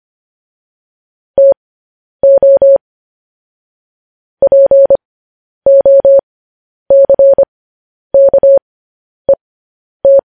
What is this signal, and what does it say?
Morse code, heard as a keyed tone, about 25 wpm, spaced as for 10 wpm, 550 Hz. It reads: TO POCKET
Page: news-headlines-in-morse-code-at-10-wpm